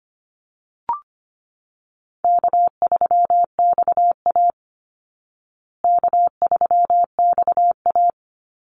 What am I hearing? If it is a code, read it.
Morse code, heard as a keyed tone, about 25 wpm, 700 Hz. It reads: K3XA K3XA
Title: Call Signs Repeated 2x - 25wpm